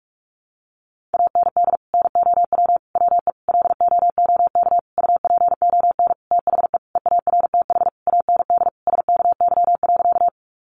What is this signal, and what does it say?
Morse code, heard as a keyed tone, about 44 wpm, 700 Hz. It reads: AND NOW WE LOOK UPON THE EARTH AND SKY.
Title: Sentences from Top 1200 Words 45wpm